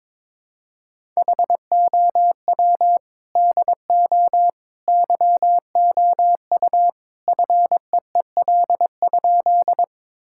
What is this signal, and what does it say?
Morse code, heard as a keyed tone, about 22 wpm, 700 Hz. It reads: HOW DO YOU FEEL?
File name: practice / 22wpm / Sentences with 3-Word Phrases - 22wpm